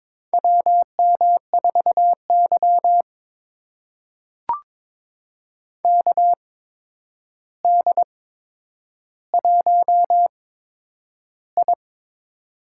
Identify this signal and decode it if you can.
Morse code, heard as a keyed tone, about 22 wpm, spaced as for 9 wpm, 700 Hz. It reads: WM4Y KD1I
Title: USA Call Signs - 4 Characters - 8x Character Spacing - 22wpm